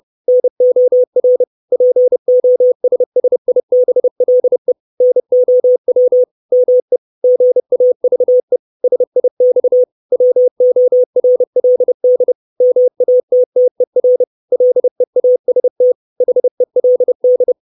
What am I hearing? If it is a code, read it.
Morse code, heard as a keyed tone, about 30 wpm, 500 Hz. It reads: NOR POSSIBLE NOW ME GAVE SIX WORLD MATTER LEAST HELD
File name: 632 Words 30WPM